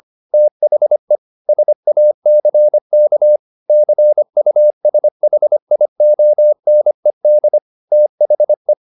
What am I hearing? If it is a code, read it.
Morse code, heard as a keyed tone, about 25 wpm, 600 Hz. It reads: THE SACK CUSHIONED THE
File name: the old man and the sea part 1 25wpm